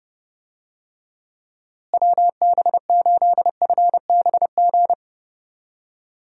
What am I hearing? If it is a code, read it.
Morse code, heard as a keyed tone, about 30 wpm, 700 Hz. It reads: WB8FBG
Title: USA Call Signs - 6 Characters - Repeated 2x - 30wpm